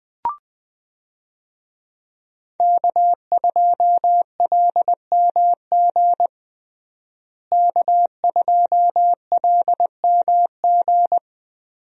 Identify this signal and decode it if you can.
Morse code, heard as a keyed tone, about 20 wpm, 700 Hz. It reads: K2LMG K2LMG
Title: USA Call Signs - 5 Characters - Repeated 3x - 20wpm